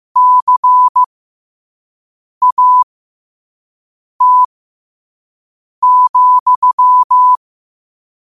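Morse code, heard as a keyed tone, about 15 wpm, spaced as for 5 wpm, 1000 Hz. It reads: CAT,
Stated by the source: Morse code sent at 5 wpm.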